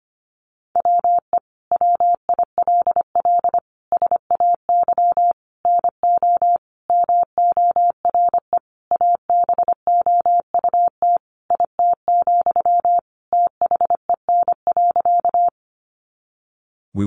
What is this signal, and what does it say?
Morse code, heard as a keyed tone, about 25 wpm, 700 Hz. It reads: WE WILL SAY NO MORE ABOUT IT, THEN.
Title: Sentences from Top 200 Words 25wpm